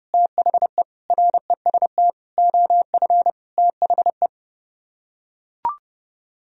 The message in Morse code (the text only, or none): THE REST OF THE